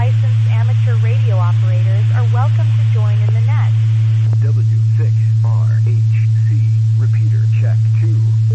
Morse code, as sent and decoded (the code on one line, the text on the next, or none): none